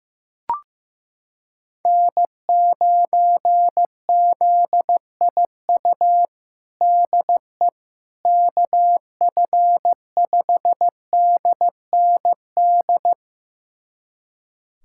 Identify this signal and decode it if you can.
Morse code, heard as a keyed tone, about 16 wpm, 700 Hz. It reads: N9ZIU DE KF5DND
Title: USA Callsign DE Callsign 15wpm